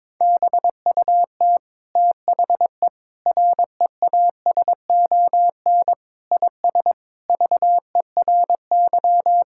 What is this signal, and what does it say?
Morse code, heard as a keyed tone, about 21 wpm, 700 Hz. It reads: BUT THE REASON IS VERY
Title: Sentences from Top 1600 Words 22wpm